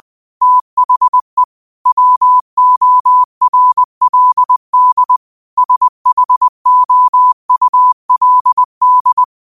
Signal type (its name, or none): Morse code